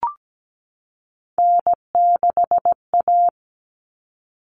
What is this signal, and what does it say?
Morse code, heard as a keyed tone, about 17 wpm, 700 Hz. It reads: N6A
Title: All Cut Numbers - 3-Digit 17wpm